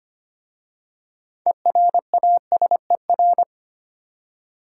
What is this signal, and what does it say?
Morse code, heard as a keyed tone, about 25 wpm, 690 Hz. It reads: ERASER
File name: practice / 25wpm / Easy Words with Hints - 25wpm